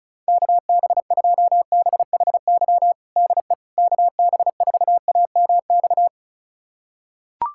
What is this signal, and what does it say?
Morse code, heard as a keyed tone, about 36 wpm, 700 Hz. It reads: KB2BHY DE KB4AMX
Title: USA Callsign DE Callsign - 6 Characters - 35wpm